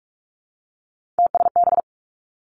Morse code, heard as a keyed tone, about 46 wpm, 700 Hz. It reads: TSB
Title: Three Challenging Letter-Number Combinations 45wpm